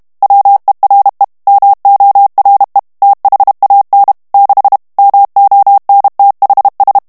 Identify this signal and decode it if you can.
Morse code, heard as a keyed tone, about 32 wpm, 800 Hz. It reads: WERE MORE THAN 6 MONTHS
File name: morse mp3 / ANGLAIS texte clair 3 vitesse 30 mots minute